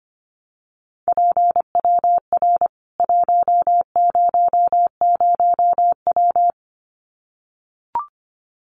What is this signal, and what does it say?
Morse code, heard as a keyed tone, about 25 wpm, 700 Hz. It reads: PWR 100W
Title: Common QSO Phrases 25wpm